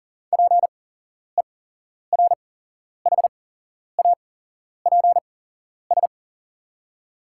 Morse code, heard as a keyed tone, about 40 wpm, spaced as for 11 wpm, 700 Hz. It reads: PERHAPS
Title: 8x Character Spacing - Top 500 Words - 40wpm